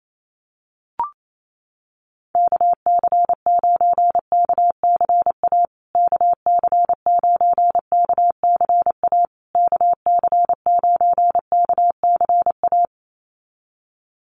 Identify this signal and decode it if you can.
Morse code, heard as a keyed tone, about 28 wpm, 700 Hz. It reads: KC9KCA KC9KCA KC9KCA
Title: USA Call Signs Repeated 3x 28wpm